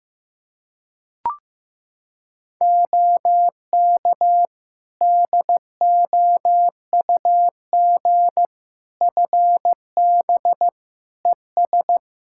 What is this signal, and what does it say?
Morse code, heard as a keyed tone, about 15 wpm, 700 Hz. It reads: OK DOUG FB ES
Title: Practice QSO - Exchange 2 - Info - 15wpm